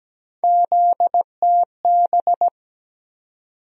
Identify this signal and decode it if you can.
Morse code, heard as a keyed tone, about 17 wpm, 700 Hz. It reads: ZTB
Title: USA Call Sign Postfixes - 3 Letters 17wpm